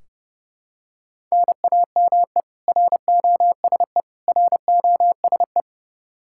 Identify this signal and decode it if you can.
Morse code, heard as a keyed tone, about 29 wpm, 700 Hz. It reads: NAME ROSE ROSE
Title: Practice QSO - Exchange 1 - Beginning - 30wpm